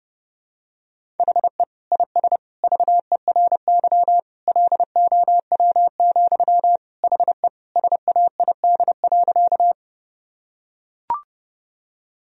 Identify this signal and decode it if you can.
Morse code, heard as a keyed tone, about 30 wpm, 700 Hz. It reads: HE IS VERY LOW, HE SAID.
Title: Sentences from Top 1000 Words - Review Part 1 - 30wpm